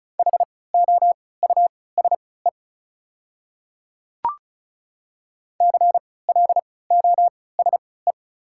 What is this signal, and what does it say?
Morse code, heard as a keyed tone, about 36 wpm, spaced as for 20 wpm, 700 Hz. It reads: HOUSE CLOSE
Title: 3x Character Spacing - Top 300 Words - 35wpm